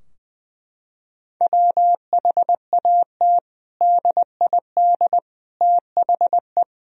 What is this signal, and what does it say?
Morse code, heard as a keyed tone, about 20 wpm, 700 Hz. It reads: WHAT DID THE
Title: Sentences from Top 1000 Words - Review Part 4 - 20wpm